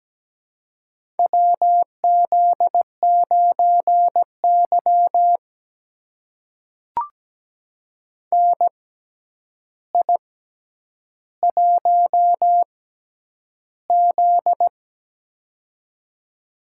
Morse code, heard as a keyed tone, about 17 wpm, spaced as for 9 wpm, 700 Hz. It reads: WZ9Y NI1Z
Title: USA Call Signs - 4 Characters - 6x Character Spacing - 17wpm